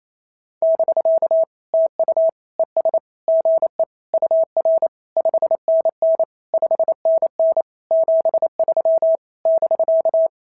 Morse code, heard as a keyed tone, about 28 wpm, 650 Hz. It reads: <BK> TU ES GE UR 5NN 5NN 73 <BK>
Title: SOTA QSO Practice 28wpm